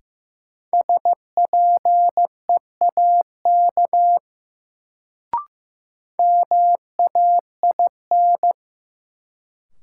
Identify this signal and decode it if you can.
Morse code, heard as a keyed tone, about 15 wpm, 700 Hz. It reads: SPEAK MAIN
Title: Sentences from Top 1000 Words - Review Part 2 - New Word Drill - 15wpm